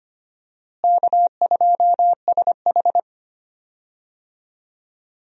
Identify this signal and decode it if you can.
Morse code, heard as a keyed tone, about 25 wpm, 700 Hz. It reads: K2SH
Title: USA Call Signs - 4 Characters - Repeated 3x - 25wpm